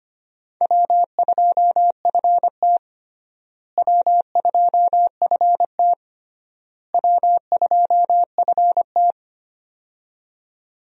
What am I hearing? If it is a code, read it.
Morse code, heard as a keyed tone, about 25 wpm, 700 Hz. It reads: W2FT W2FT W2FT
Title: USA Call Signs - 4 Characters - Repeated 3x - 25wpm